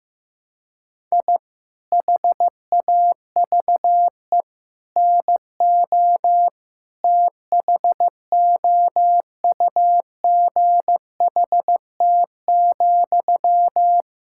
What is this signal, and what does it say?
Morse code, heard as a keyed tone, about 15 wpm, 700 Hz. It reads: I HAVE NO THOUGHT,
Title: Sentences from Top 1000 Words - Review Part 6 - 15wpm